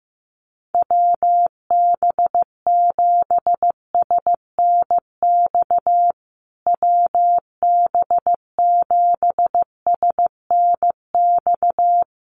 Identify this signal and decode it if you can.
Morse code, heard as a keyed tone, about 15 wpm, 700 Hz. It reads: WB7SNX WB7SNX
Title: USA Call Signs Repeated 3x 15wpm